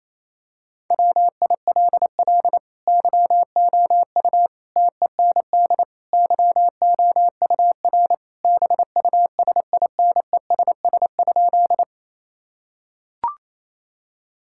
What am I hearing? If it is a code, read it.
Morse code, heard as a keyed tone, about 28 wpm, 700 Hz. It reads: WILL YOU TEND YOUR BUSINESS?
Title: Sentences from Top 1000 Words - Review Part 5 - 28wpm